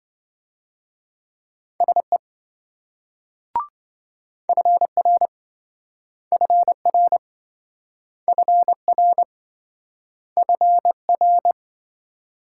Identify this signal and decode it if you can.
Morse code, heard as a keyed tone, about 26 wpm, 700 Hz. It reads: SE FR FR FR FR